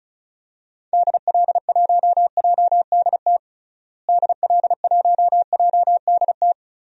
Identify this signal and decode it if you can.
Morse code, heard as a keyed tone, about 36 wpm, 700 Hz. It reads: DL1JDT DL1JDT
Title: DXCC-100 Call Signs - 6 Characters - Repeated 3x - 35wpm